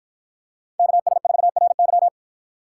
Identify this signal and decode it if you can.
Morse code, heard as a keyed tone, about 50 wpm, 700 Hz. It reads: KS4RY